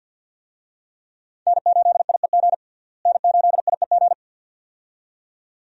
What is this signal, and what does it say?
Morse code, heard as a keyed tone, about 48 wpm, 700 Hz. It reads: N8IEG N8IEG
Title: USA Call Signs - 5 Characters - Repeated 2x - 50wpm